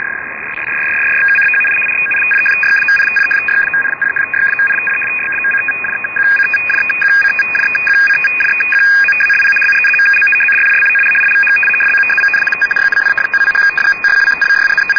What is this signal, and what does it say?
Two-tone teleprinter signal, about 850 Hz shift.
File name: uploader / 11214 francia